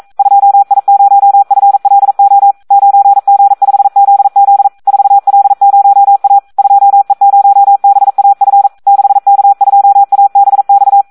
Morse code, heard as a keyed tone, about 42 wpm, 800 Hz. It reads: 2I0FLO 9G5ZZ 4L0A 2E0BAF 6K2ABX
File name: Audio practica CW / SD 1 42